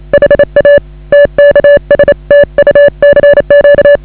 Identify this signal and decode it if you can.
Morse code, heard as a keyed tone, about 34 wpm, 560 Hz. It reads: HA TKS TU CQ
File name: AUDIO2019 / HA8KAZ CW